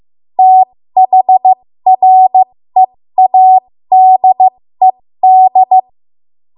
Morse code, heard as a keyed tone, about 15 wpm, spaced as for 13 wpm, 750 Hz. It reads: THREADED
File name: Morse / 13WPM